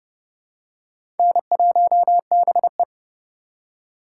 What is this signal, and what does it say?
Morse code, heard as a keyed tone, about 31 wpm, 700 Hz. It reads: N1BE
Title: USA Call Signs - 4 Characters - 4x Character Spacing - 30wpm